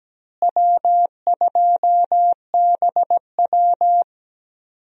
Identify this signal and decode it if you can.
Morse code, heard as a keyed tone, about 17 wpm, 700 Hz. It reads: W2BW
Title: USA Call Signs - 4 Characters - Repeated 3x - 17wpm